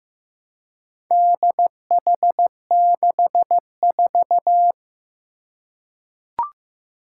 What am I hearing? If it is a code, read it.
Morse code, heard as a keyed tone, about 15 wpm, 700 Hz. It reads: DH64
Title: Four Challenging Letter-Number Combinations 15wpm